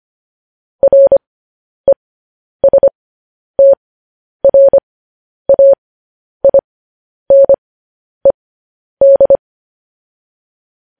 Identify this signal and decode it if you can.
Morse code, heard as a keyed tone, about 25 wpm, spaced as for 10 wpm, 550 Hz. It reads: RESTRAINED
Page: news headlines in morse code at 10 wpm